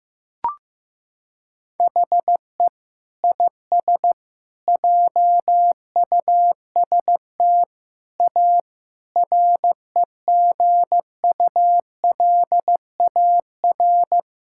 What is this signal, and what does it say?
Morse code, heard as a keyed tone, about 15 wpm, 700 Hz. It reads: HE IS JUST A REGULAR
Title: Sentences from Top 1000 Words - Review Part 6 - 15wpm